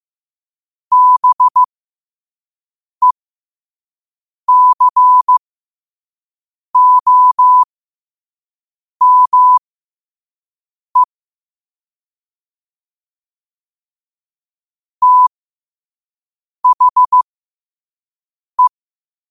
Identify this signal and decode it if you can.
Morse code, heard as a keyed tone, about 15 wpm, spaced as for 5 wpm, 1000 Hz. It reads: BECOME THE
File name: qotdfiles / 5wpm